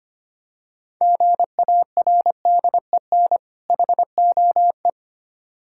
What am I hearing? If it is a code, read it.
Morse code, heard as a keyed tone, about 26 wpm, 700 Hz. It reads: GARDEN HOE
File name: practice / 25wpm / Difficult Words - Speed-Racing with Hints - 25wpm